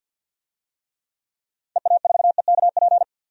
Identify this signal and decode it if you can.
Morse code, heard as a keyed tone, about 52 wpm, 700 Hz. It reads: EA4EKP
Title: DXCC-100 Call Signs - 6 Characters - 50wpm